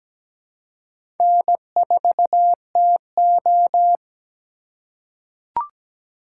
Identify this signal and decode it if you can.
Morse code, heard as a keyed tone, about 17 wpm, 700 Hz. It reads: N4TO
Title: USA Call Signs - 4 Characters - 3x Character Spacing - 17wpm